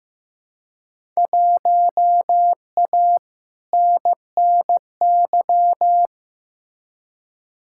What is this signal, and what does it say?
Morse code, heard as a keyed tone, about 15 wpm, 700 Hz. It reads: 1A NNY
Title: Field Day Exchanges Encore - 15wpm